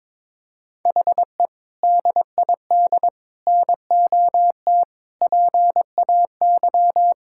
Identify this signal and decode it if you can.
Morse code, heard as a keyed tone, about 22 wpm, 700 Hz. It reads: HE DID NOT PAY
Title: Sentences from Top 1000 Words - Review Part 6 - 22wpm